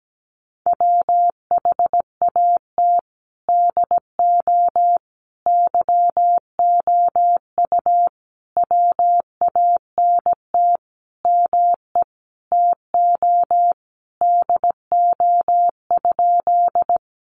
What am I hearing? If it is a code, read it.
Morse code, heard as a keyed tone, about 17 wpm, 700 Hz. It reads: WHAT DO YOU WANT ME TO DO?
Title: Sentences from Top 200 Words 17wpm